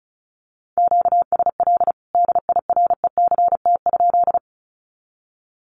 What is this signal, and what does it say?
Morse code, heard as a keyed tone, about 34 wpm, 700 Hz. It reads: QSL DIRECT?
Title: Common QSO Phrases 35wpm